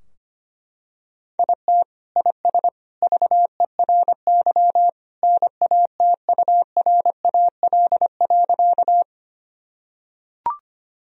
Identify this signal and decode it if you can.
Morse code, heard as a keyed tone, about 25 wpm, 700 Hz. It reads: IT IS VERY NATURAL.
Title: Sentences from Top 1000 Words - Review Part 4 - 25wpm